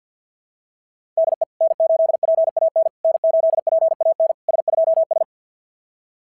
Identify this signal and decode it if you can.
Morse code, heard as a keyed tone, about 52 wpm, 660 Hz. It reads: DE N8PAN N8PAN S2S
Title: SOTA QSO Practice 50wpm